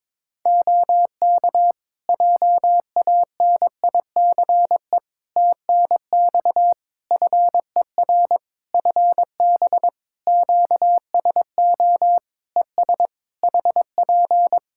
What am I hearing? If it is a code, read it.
Morse code, heard as a keyed tone, about 22 wpm, 700 Hz. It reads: OK JANICE TNX FER FB QSO ES HP
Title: K1USN SST QSO Practice 22wpm